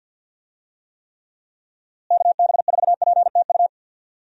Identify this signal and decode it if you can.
Morse code, heard as a keyed tone, about 52 wpm, 700 Hz. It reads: KB4PTU
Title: USA Call Signs - 6 Characters - Repeated 2x - 50wpm